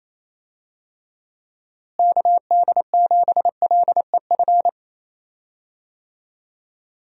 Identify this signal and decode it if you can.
Morse code, heard as a keyed tone, about 27 wpm, 700 Hz. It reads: KD7LEF
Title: USA Call Signs - 6 Characters - Repeated 2x - 28wpm